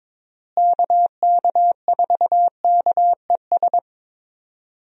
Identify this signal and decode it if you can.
Morse code, heard as a keyed tone, about 22 wpm, 700 Hz. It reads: KK4KES
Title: USA Call Signs - 6 Characters - Repeated 2x - 22wpm